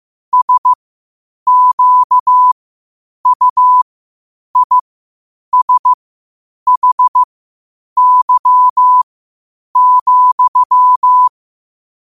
Morse code, heard as a keyed tone, about 15 wpm, spaced as for 8 wpm, 1000 Hz. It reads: SQUISHY,
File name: qotdfiles / 8wpm